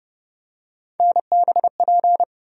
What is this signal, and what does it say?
Morse code, heard as a keyed tone, about 30 wpm, 700 Hz. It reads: NBP
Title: USA Call Sign Postfixes - 3 Letters 30wpm